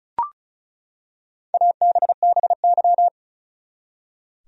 Morse code, heard as a keyed tone, about 36 wpm, 700 Hz. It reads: ABBY
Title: USA Common Names - Max 5 Letters - 35wpm